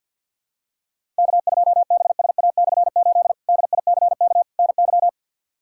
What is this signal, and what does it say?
Morse code, heard as a keyed tone, about 51 wpm, 700 Hz. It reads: K2BSA/8 DICK NY
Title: K1USN SST QSO Practice - 50wpm